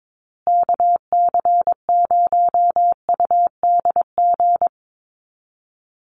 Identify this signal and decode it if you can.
Morse code, heard as a keyed tone, about 22 wpm, 700 Hz. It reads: KC0UDG
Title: USA Call Signs Repeated 3x 22wpm